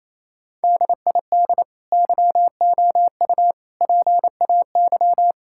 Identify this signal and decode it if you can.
Morse code, heard as a keyed tone, about 28 wpm, 700 Hz. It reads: DID YOU PAY
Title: Sentences from Top 1000 Words - Review Part 5 - 28wpm